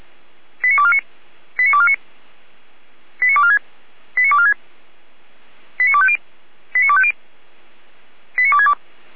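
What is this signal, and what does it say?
Five-tone selective call (ZVEI), heard as quick runs of five tones, tones of about 70 ms. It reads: 89249 89249 89246 89246 89250 89250 89262
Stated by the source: ZVEI five-tone call 89163 received by SDR.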